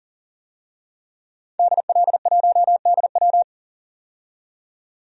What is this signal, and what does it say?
Morse code, heard as a keyed tone, about 40 wpm, 700 Hz. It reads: DL1DW
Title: DXCC-100 Call Signs - 5 Characters - 40wpm